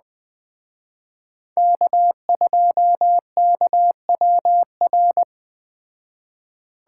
Morse code, heard as a keyed tone, about 19 wpm, 700 Hz. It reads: K2KWR